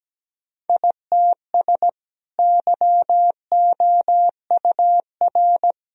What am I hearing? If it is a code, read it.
Morse code, heard as a keyed tone, about 17 wpm, 700 Hz. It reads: ITS YOUR
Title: Sentences from Top 1000 Words - Review Part 1 - 17wpm